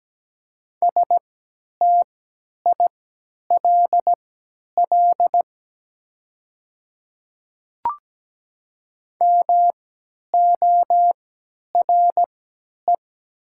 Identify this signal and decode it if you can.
Morse code, heard as a keyed tone, about 17 wpm, spaced as for 10 wpm, 700 Hz. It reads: STILL MORE
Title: 3x Character Spacing - Top 300 Words - 17wpm